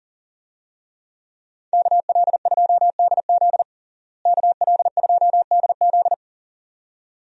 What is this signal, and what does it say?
Morse code, heard as a keyed tone, about 37 wpm, 700 Hz. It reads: KL2DZ KL2DZ
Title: USA Call Signs - 5 Characters - Repeated 2x - 40wpm